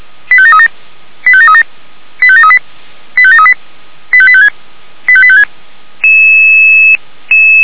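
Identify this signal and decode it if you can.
Five-tone selective call (ZVEI), heard as quick runs of five tones, tones of about 70 ms. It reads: 85737 85737 85738 85738 85856 85856
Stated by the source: ZVEI five-tone call received by SDR.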